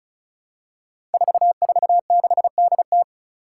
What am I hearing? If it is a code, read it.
Morse code, heard as a keyed tone, about 34 wpm, 700 Hz. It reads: 446DT
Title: Five Challenging Letter-Number Combinations 35wpm